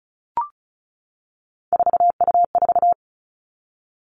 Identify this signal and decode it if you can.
Morse code, heard as a keyed tone, about 36 wpm, 700 Hz. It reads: 4U4
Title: Three Challenging Letter-Number Combinations 35wpm